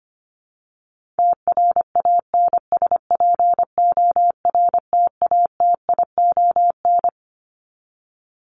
Morse code, heard as a keyed tone, about 25 wpm, 700 Hz. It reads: TRANSPORTATION
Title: Top 1501-1750 Words 25wpm